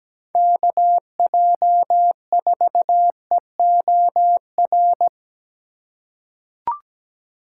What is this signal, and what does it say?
Morse code, heard as a keyed tone, about 17 wpm, 700 Hz. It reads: KJ4EOR
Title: USA Call Signs - 6 Characters - 3x Character Spacing - 17wpm